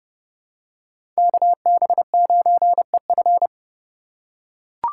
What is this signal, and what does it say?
Morse code, heard as a keyed tone, about 30 wpm, 700 Hz. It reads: KB9EF
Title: USA Call Signs - 5 Characters - 2x Character Spacing - 30wpm